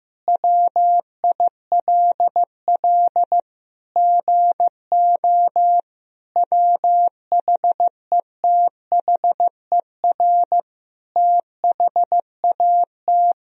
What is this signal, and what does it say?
Morse code, heard as a keyed tone, about 15 wpm, 700 Hz. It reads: WILL GO WHETHER THAT
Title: Sentences from Top 1000 Words - Review Part 3 - 15wpm